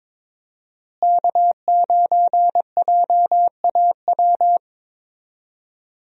Morse code, heard as a keyed tone, about 21 wpm, 700 Hz. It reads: K9JAW
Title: USA Call Signs - 5 Characters - Repeated 3x - 22wpm